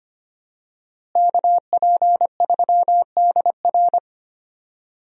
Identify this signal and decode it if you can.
Morse code, heard as a keyed tone, about 25 wpm, 700 Hz. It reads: KP3DR
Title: USA Call Signs - 5 Characters - Repeated 2x - 25wpm